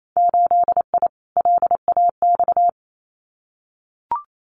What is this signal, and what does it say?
Morse code, heard as a keyed tone, about 28 wpm, 690 Hz. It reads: 8I LAX
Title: Winter Field Day Exchanges Encore 28wpm